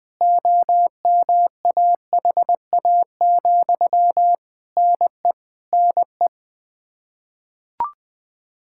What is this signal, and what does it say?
Morse code, heard as a keyed tone, about 20 wpm, 700 Hz. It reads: OMAHA, NE NE
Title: Practice QSO - Exchange 1 - Beginning - 20wpm